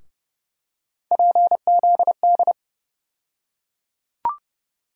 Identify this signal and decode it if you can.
Morse code, heard as a keyed tone, about 30 wpm, 700 Hz. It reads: PZD
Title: USA Call Sign Postfixes - 3 Letters 30wpm